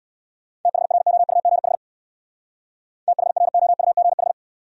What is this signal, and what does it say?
Morse code, heard as a keyed tone, about 65 wpm, 700 Hz. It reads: THROUGH THROUGH